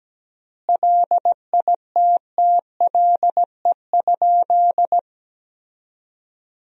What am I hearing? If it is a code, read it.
Morse code, heard as a keyed tone, about 17 wpm, 700 Hz. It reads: LITTLE?
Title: 3x Word-Spacing - Sentences from Top 500 Words - 17wpm